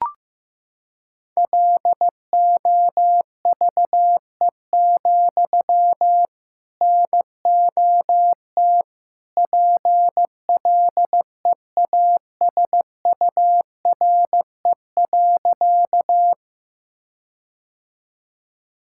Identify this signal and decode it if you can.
Morse code, heard as a keyed tone, about 15 wpm, 700 Hz. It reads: LOVE, NOT PLEASURE.
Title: Sentences from Top 1600 Words 15wpm